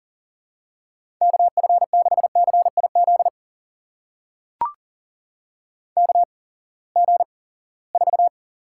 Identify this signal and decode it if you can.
Morse code, heard as a keyed tone, about 39 wpm, spaced as for 23 wpm, 700 Hz. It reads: KF6CIZ KG4
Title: USA Call Signs - 6 Characters - 8x Character Spacing - 40wpm